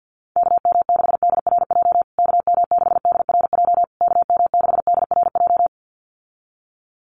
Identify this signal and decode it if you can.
Morse code, heard as a keyed tone, about 50 wpm, 700 Hz. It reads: KM6DRJ KM6DRJ KM6DRJ
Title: USA Call Signs Repeated 3x 50wpm